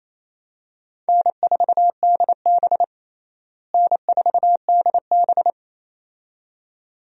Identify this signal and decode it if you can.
Morse code, heard as a keyed tone, about 28 wpm, 700 Hz. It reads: N4DB N4DB
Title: USA Call Signs - 4 Characters - Repeated 2x - 28wpm